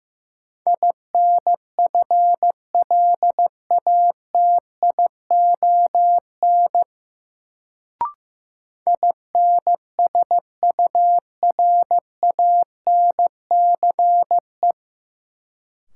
Morse code, heard as a keyed tone, about 15 wpm, 700 Hz. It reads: INFLATION INSURANCE
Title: Sentences from Top 1100 Words - New Word Drill Alt - 15wpm